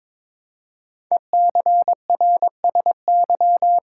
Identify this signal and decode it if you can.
Morse code, heard as a keyed tone, about 22 wpm, 700 Hz. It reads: ECRSY
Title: Code Groups - 5 Letters - 22wpm